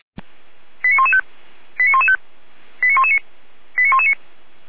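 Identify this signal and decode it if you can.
Five-tone selective call (ZVEI), heard as quick runs of five tones, tones of about 70 ms. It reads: 89115 89115 89119 89119
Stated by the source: ZVEI five-tone selective call received by SDR, fire brigade Orlamünde (feuerwehr-orlamuende).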